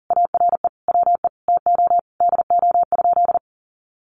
Morse code, heard as a keyed tone, about 40 wpm, 700 Hz. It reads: ARE WE TO DO?
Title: Sentences from Top 100 Words 40wpm